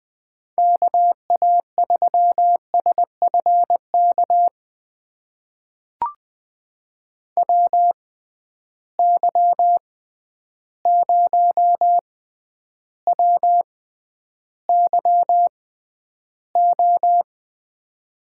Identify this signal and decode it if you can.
Morse code, heard as a keyed tone, about 20 wpm, spaced as for 10 wpm, 700 Hz. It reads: KA3SFK WY0WYO
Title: USA Call Signs - 6 Characters - 6x Character Spacing - 20wpm